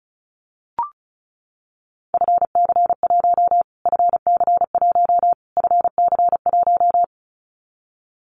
Morse code, heard as a keyed tone, about 34 wpm, 700 Hz. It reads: FC1 FC1 FC1
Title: Callsign Prefixes for Top 25 DXCC - Repeated 3x - 35wpm